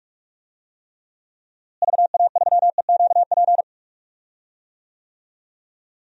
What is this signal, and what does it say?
Morse code, heard as a keyed tone, about 46 wpm, 700 Hz. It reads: VA3EQP
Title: DXCC-100 Call Signs - 6 Characters - Repeated 3x - 45wpm